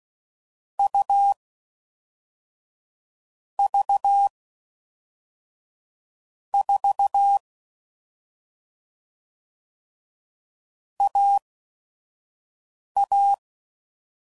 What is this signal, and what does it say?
Morse code, heard as a keyed tone, about 16 wpm, spaced as for 5 wpm, 800 Hz. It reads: U V 4 AA
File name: Combinations / Difficult Combos / AUV AUV4